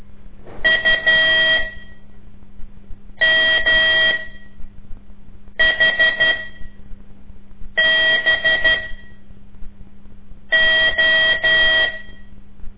Morse code, heard as a keyed tone, about 12 wpm, spaced as for 5 wpm, 1900 Hz. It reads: UMHBO
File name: ik1weg2 / CORSI CW / lez5-2